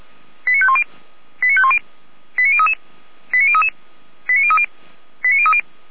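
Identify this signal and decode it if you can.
Five-tone selective call (ZVEI), heard as quick runs of five tones, tones of about 70 ms. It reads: 89510 89510 89033 89033 89030 89030
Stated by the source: ZVEI five-tone selective call received by SDR, from the feuerwehr-orlamuende fire brigade audio server.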